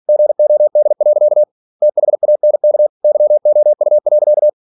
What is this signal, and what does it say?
Morse code, heard as a keyed tone, about 46 wpm, 600 Hz. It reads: GOD. THANK YOU.
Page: Never 47wpm